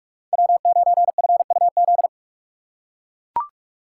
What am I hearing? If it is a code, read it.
Morse code, heard as a keyed tone, about 41 wpm, 700 Hz. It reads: W9FUZ
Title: USA Call Signs - 5 Characters - 4x Character Spacing - 45wpm